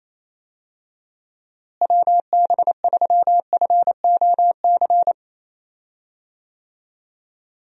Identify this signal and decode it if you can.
Morse code, heard as a keyed tone, about 28 wpm, 700 Hz. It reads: WB3FOC
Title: USA Call Signs - 6 Characters - Repeated 2x - 28wpm